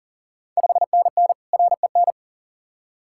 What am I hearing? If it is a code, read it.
Morse code, heard as a keyed tone, about 40 wpm, 690 Hz. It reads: 5NN REN
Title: Ontario QSO Party Exchange 40wpm